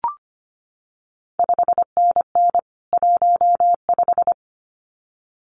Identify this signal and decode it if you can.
Morse code, heard as a keyed tone, about 25 wpm, 700 Hz. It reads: 5NN 15
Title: CQ WW DX Contest Exchange - 25wpm